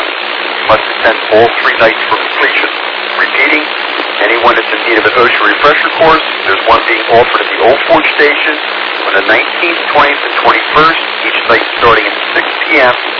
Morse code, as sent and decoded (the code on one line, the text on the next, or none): none